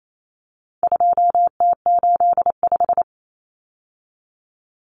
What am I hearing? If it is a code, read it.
Morse code, heard as a keyed tone, about 28 wpm, 700 Hz. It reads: 2T85
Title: Common Cut Numbers - 4-Digit 28wpm